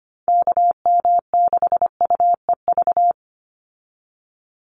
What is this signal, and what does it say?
Morse code, heard as a keyed tone, about 25 wpm, 700 Hz. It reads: KM6UEV
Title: USA Call Signs 25wpm